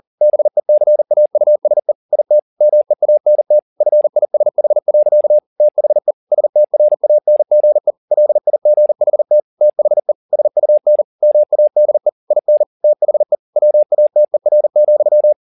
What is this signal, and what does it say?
Morse code, heard as a keyed tone, about 40 wpm, 600 Hz. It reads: BECAUSE IT MEANT FISH. THE STRANGE LIGHT THE SUN MADE IN THE WATER,